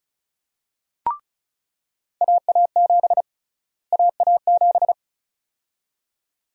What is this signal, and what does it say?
Morse code, heard as a keyed tone, about 36 wpm, 700 Hz. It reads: AA7 AA7
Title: USA Call Sign Prefixes - Repeated 2x - 35wpm